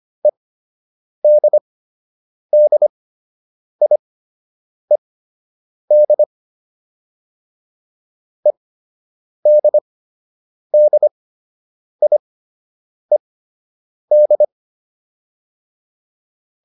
Morse code, heard as a keyed tone, about 25 wpm, spaced as for 8 wpm, 600 Hz. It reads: EDDIED EDDIED
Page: fundamental-practice-files